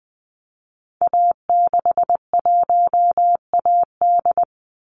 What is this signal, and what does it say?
Morse code, heard as a keyed tone, about 20 wpm, 700 Hz. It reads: A61AD
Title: Call Signs Repeated 3x - 20wpm